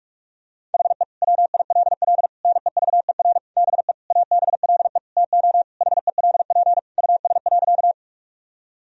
Morse code, heard as a keyed tone, about 46 wpm, 700 Hz. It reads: HE WILL NEVER BE ABLE TO HELP US.
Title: Sentences from Top 1000 Words - Review Part 2 - 45wpm